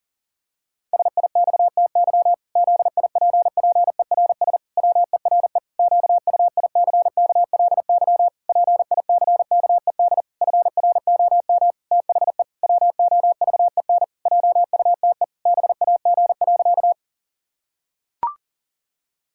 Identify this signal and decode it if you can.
Morse code, heard as a keyed tone, about 40 wpm, 700 Hz. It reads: SIXTY ZIPPERS WERE QUICKLY PICKED FROM THE WOVEN JUTE BAG.
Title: Pangram Challenge v2 - 40wpm